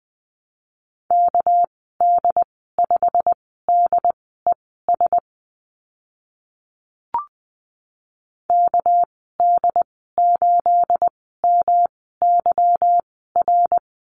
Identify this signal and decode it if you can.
Morse code, heard as a keyed tone, about 20 wpm, spaced as for 14 wpm, 700 Hz. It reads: KD5DES KD8MYR
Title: 219 Lesson 35g - Number 0 - Call Signs 20wpm